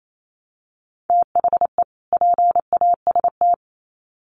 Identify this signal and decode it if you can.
Morse code, heard as a keyed tone, about 28 wpm, 700 Hz. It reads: THE PAST
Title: Sets of 2 Words 28wpm